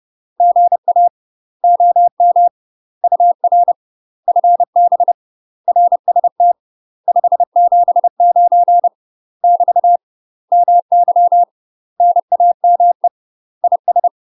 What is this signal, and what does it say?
Morse code, heard as a keyed tone, about 30 wpm, 700 Hz. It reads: GA OM UR FB RST 579 <BT> MY NAME IS
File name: UK MASTER QSOs TRAINING/30WPM